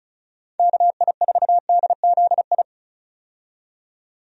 Morse code, heard as a keyed tone, about 36 wpm, 700 Hz. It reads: KI4DZI
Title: USA Call Signs - 6 Characters - 35wpm